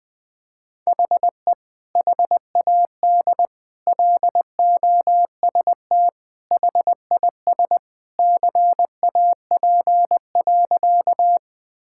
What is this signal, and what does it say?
Morse code, heard as a keyed tone, about 20 wpm, 700 Hz. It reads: HE HAD LOST HIS CAP.
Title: Sentences from Top 1600 Words 20wpm